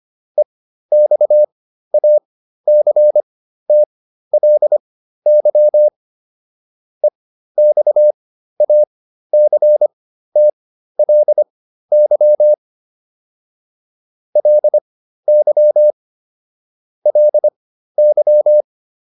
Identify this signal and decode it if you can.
Morse code, heard as a keyed tone, about 25 wpm, spaced as for 13 wpm, 600 Hz. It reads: EXACTLY EXACTLY LY LY
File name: LY7 13